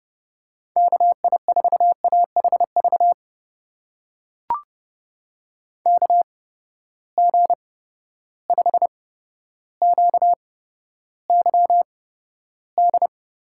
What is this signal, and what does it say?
Morse code, heard as a keyed tone, about 30 wpm, spaced as for 13 wpm, 700 Hz. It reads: KI4AHV KG5QYD
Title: USA Call Signs - 6 Characters - 8x Character Spacing - 30wpm